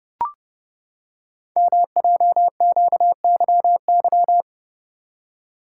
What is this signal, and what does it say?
Morse code, heard as a keyed tone, about 30 wpm, 700 Hz. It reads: MJQYY
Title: Code Groups - 5 Letters - 30wpm